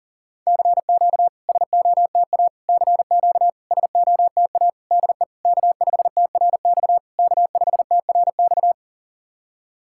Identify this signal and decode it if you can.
Morse code, heard as a keyed tone, about 40 wpm, 700 Hz. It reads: CQ SOTA CQ SOTA DE K5TRX K5TRX
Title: SOTA QSO Practice 40wpm